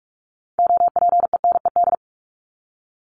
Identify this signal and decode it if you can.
Morse code, heard as a keyed tone, about 44 wpm, 700 Hz. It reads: OPENED
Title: ED Suffix - Top 100 Words with Base Words 45wpm